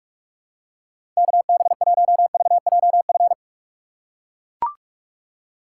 Morse code, heard as a keyed tone, about 43 wpm, 700 Hz. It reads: KB1VJF
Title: USA Call Signs - 6 Characters - 3x Character Spacing - 45wpm